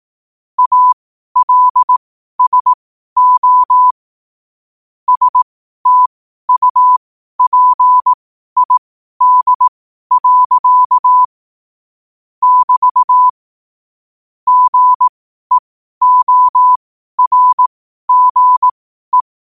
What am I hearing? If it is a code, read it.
Morse code, heard as a keyed tone, about 18 wpm, spaced as for 12 wpm, 1000 Hz. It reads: ALSO STUPID. <BT> GEORGE
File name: qotdfiles / 12wpm